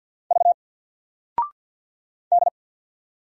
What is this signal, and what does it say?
Morse code, heard as a keyed tone, about 50 wpm, 700 Hz. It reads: V D
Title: Single Letter-Number 50wpm